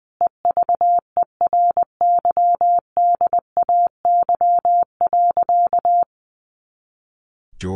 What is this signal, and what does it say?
Morse code, heard as a keyed tone, about 20 wpm, 700 Hz. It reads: EVERYDAY.
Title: Additional Easy Phrases 20wpm